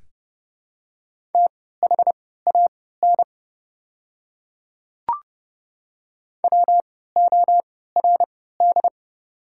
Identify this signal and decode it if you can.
Morse code, heard as a keyed tone, about 30 wpm, spaced as for 17 wpm, 700 Hz. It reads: THAN WORD
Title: 3x Character Spacing - Top 200 Words - 30wpm